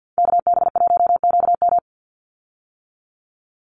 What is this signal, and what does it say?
Morse code, heard as a keyed tone, about 50 wpm, 700 Hz. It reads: KB1QM
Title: USA Call Signs 50wpm